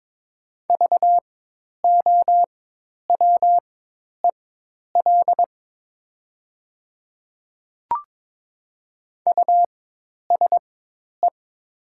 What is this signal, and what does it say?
Morse code, heard as a keyed tone, about 22 wpm, spaced as for 10 wpm, 700 Hz. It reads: VOWEL USE
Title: 4x Character Spacing - Top 500 Words - 22wpm